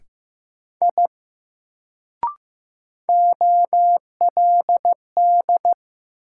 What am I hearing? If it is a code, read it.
Morse code, heard as a keyed tone, about 15 wpm, 700 Hz. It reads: I OLD
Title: Sentences from Top 200 Words - New Word Drill - 15wpm